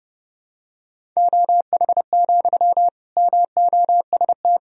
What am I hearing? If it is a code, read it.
Morse code, heard as a keyed tone, about 32 wpm, 700 Hz. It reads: OH, MOST
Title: Sentences from Top 1000 Words - Review Part 1 - 30wpm